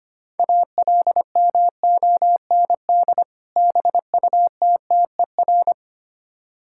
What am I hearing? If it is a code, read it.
Morse code, heard as a keyed tone, about 25 wpm, 700 Hz. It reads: ALMOND BUTTER
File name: practice / 25wpm / Difficult Words with Hints - 25wpm